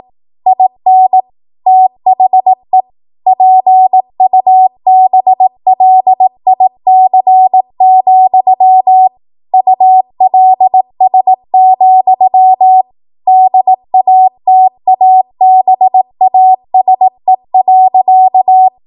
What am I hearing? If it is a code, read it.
Morse code, heard as a keyed tone, about 18 wpm, 750 Hz. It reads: IN THE PUBLIC, ULS, DATABASE.
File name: Morse / 18WPM